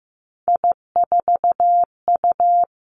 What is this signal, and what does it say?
Morse code, heard as a keyed tone, about 16 wpm, 700 Hz. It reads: I4U
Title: Three Challenging Letter-Number Combinations 15wpm